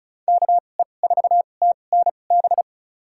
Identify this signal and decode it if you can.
Morse code, heard as a keyed tone, about 36 wpm, spaced as for 25 wpm, 700 Hz. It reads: KE4TNB
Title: USA Call Signs - 6 Characters - 2x Character Spacing - 35wpm